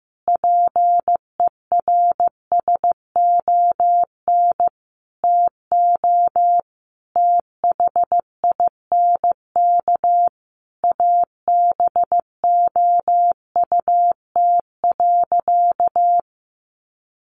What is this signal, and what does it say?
Morse code, heard as a keyed tone, about 15 wpm, 700 Hz. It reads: PERSON TO THINK ABOUT.
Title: Sentences from Top 1100 Words 15wpm